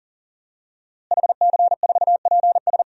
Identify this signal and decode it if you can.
Morse code, heard as a keyed tone, about 40 wpm, 700 Hz. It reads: HC4PS
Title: DXCC-100 Call Signs - 5 Characters - Repeated 3x - 40wpm